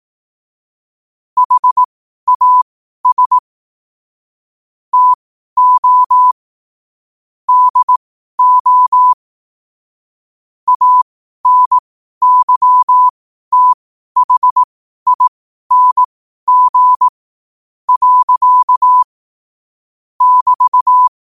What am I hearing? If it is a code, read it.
Morse code, heard as a keyed tone, about 18 wpm, spaced as for 12 wpm, 1000 Hz. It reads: HAS TO DO ANYTHING. <BT>
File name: qotdfiles / 12wpm